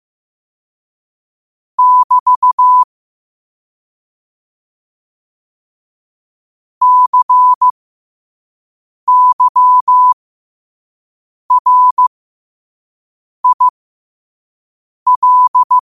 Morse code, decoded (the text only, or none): <BT> CYRIL